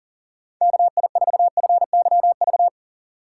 Morse code, heard as a keyed tone, about 50 wpm, 700 Hz. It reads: KI4FYV
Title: Speed Racing - USA Call Signs 40wpm